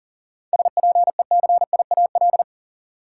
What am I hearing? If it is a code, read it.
Morse code, heard as a keyed tone, about 40 wpm, 700 Hz. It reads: SPECIAL
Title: Seven Letter Words - 40wpm